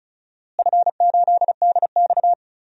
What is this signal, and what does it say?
Morse code, heard as a keyed tone, about 36 wpm, 700 Hz. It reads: F8DX
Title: DXCC-100 Call Signs - 4 Characters - Repeated 3x - 35wpm